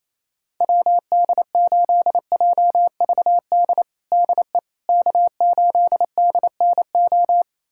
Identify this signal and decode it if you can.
Morse code, heard as a keyed tone, about 28 wpm, 700 Hz. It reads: WD8JVD DE K8DNO
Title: USA Callsign DE Callsign 28wpm